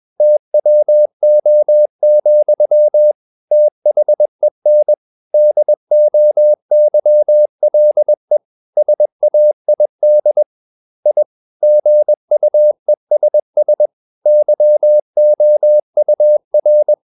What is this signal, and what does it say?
Morse code, heard as a keyed tone, about 21 wpm, 600 Hz. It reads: TWO, THEN DOYLE SAID I GUESS YOUR
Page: CW med Gnister